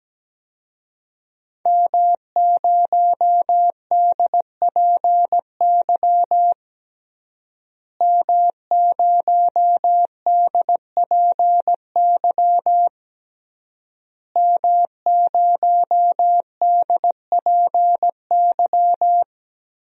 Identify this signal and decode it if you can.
Morse code, heard as a keyed tone, about 17 wpm, 700 Hz. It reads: M0DPY M0DPY M0DPY
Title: DXCC-100 Call Signs - 5 Characters - Repeated 3x - 17wpm